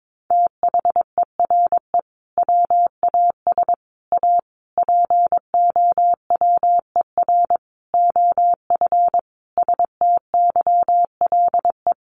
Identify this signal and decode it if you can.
Morse code, heard as a keyed tone, about 22 wpm, 700 Hz. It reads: THERE WAS A POWER OF STYLE
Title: Sentences from Top 800 Words 22wpm